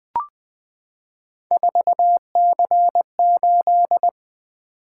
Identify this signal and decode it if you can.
Morse code, heard as a keyed tone, about 20 wpm, 700 Hz. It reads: 4C8
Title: Three Letter-Number Combinations - 20wpm